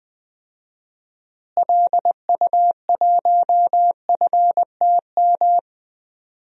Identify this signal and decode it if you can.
Morse code, heard as a keyed tone, about 20 wpm, 700 Hz. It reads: LU1FTM
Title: DXCC-100 Call Signs - 6 Characters - Repeated 3x - 20wpm